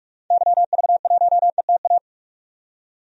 Morse code, heard as a keyed tone, about 42 wpm, 700 Hz. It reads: YV1ETA
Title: DXCC-100 Call Signs - 6 Characters - 45wpm